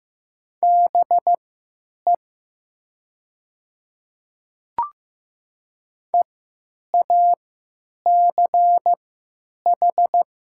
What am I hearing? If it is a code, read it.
Morse code, heard as a keyed tone, about 15 wpm, spaced as for 9 wpm, 690 Hz. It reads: BE EACH
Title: 3x Character Spacing - Top 100 Words - 15wpm